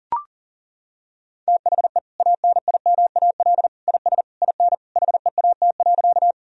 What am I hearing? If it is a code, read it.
Morse code, heard as a keyed tone, about 37 wpm, 700 Hz. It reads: THE ANIMAL IS IN HEAT.
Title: Sentences from Top 1000 Words - Review Part 3 - 40wpm